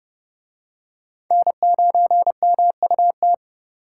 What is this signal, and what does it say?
Morse code, heard as a keyed tone, about 29 wpm, 700 Hz. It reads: N9MUT